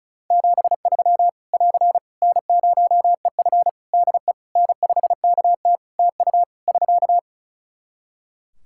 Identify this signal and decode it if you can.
Morse code, heard as a keyed tone, about 36 wpm, 700 Hz. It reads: 73 <AR> N0EF DE N5KT TU <SK>
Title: Practice QSO - Exchange 4 - Ending - 35wpm